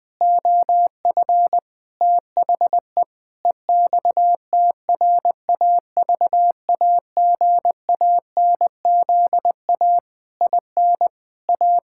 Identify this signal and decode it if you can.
Morse code, heard as a keyed tone, about 20 wpm, 700 Hz. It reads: OF THE EXTRAVAGANZA IN A
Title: Pangram Challenge v2 - 20wpm